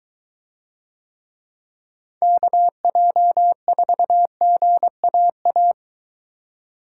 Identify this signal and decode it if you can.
Morse code, heard as a keyed tone, about 23 wpm, 700 Hz. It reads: KJ4GAA